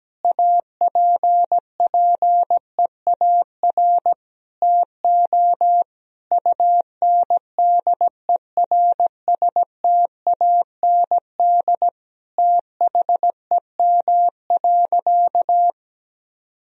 Morse code, heard as a keyed tone, about 17 wpm, 700 Hz. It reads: APPEAR TO UNDERSTAND THEM.
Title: Sentences from Top 1000 Words - Review Part 2 - 17wpm